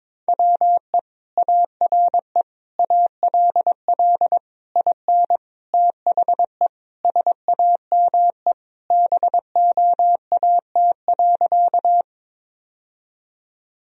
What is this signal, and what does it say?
Morse code, heard as a keyed tone, about 22 wpm, 700 Hz. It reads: WE ARE ALL IN THE SAME BOAT.
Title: Sentences from Top 1000 Words - Review Part 3 - 22wpm